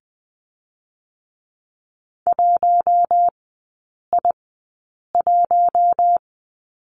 Morse code, heard as a keyed tone, about 20 wpm, spaced as for 8 wpm, 700 Hz. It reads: 1I1
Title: 041 Lesson 8e - Number 1 - Three Characters 20wpm